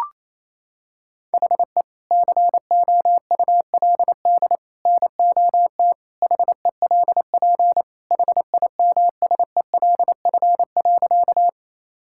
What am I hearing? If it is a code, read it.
Morse code, heard as a keyed tone, about 28 wpm, 700 Hz. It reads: HE COULD NOT HELP HIMSELF.
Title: Sentences from Top 1600 Words 28wpm